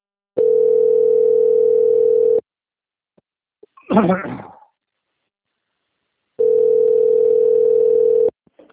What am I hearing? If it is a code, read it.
Ringback tone (North America), 440 + 480 Hz.